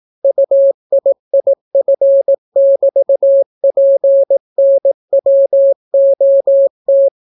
Morse code, heard as a keyed tone, about 18 wpm, 550 Hz. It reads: UIIF<BT>PNWOT